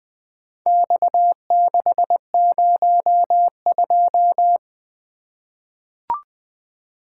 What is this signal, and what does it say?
Morse code, heard as a keyed tone, about 20 wpm, 700 Hz. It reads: X602
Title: Four Letter-Number Combinations - 20wpm